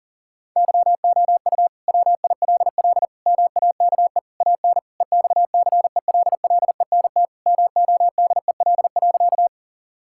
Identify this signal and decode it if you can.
Morse code, heard as a keyed tone, about 40 wpm, 700 Hz. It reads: YOU WILL MAKE AN EXCELLENT MODEL.
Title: Sentences from Top 1000 Words - Review Part 5 - 40wpm